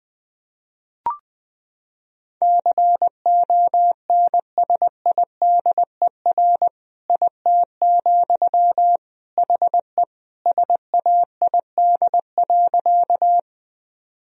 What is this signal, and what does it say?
Morse code, heard as a keyed tone, about 21 wpm, 700 Hz. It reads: CONSIDER IT, HE SAID.
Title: Sentences from Top 1000 Words - Review Part 4 - 20wpm